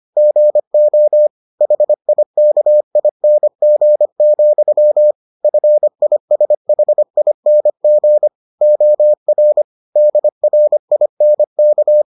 Morse code, heard as a keyed tone, about 25 wpm, 600 Hz. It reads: GO HIKING, FISHING OR DRINK